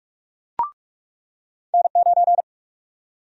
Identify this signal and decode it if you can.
Morse code, heard as a keyed tone, about 44 wpm, 700 Hz. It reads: N9
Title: USA Call Sign Prefixes - 45wpm